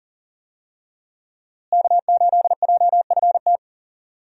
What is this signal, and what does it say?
Morse code, heard as a keyed tone, about 40 wpm, 700 Hz. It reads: K8JFT
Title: USA Call Signs - 5 Characters - Repeated 2x - 40wpm